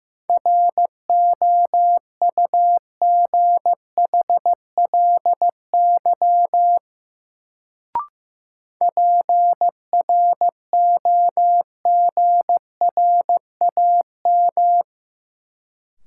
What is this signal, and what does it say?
Morse code, heard as a keyed tone, about 15 wpm, 700 Hz. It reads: ROUGHLY PROGRAM
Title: Sentences from Top 1500 Words - New Word Drill Alt - 15wpm